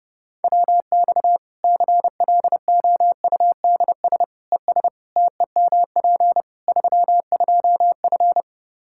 Morse code, heard as a keyed tone, about 30 wpm, 700 Hz. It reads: WX CLOUDS ES TEMP 32F
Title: Practice QSO - Exchange 2 - Info - 30wpm